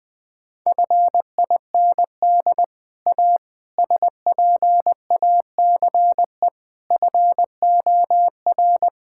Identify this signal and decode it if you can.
Morse code, heard as a keyed tone, about 20 wpm, 700 Hz. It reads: FIND A SPACE FOR
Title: Sentences from Top 1000 Words - Review Part 5 - 20wpm